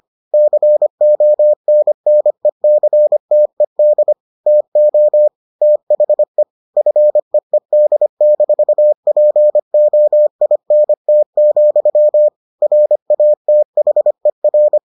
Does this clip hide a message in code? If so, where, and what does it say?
Yes: Morse code from 0.3 to 14.8 s, heard as a keyed tone, about 25 wpm, 600 Hz. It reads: CONNECTED TO THE FEED-POINT, RATHER